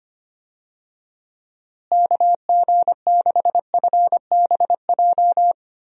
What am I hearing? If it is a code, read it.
Morse code, heard as a keyed tone, about 25 wpm, 700 Hz. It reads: KG6FBJ
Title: USA Call Signs - 6 Characters - Repeated 2x - 25wpm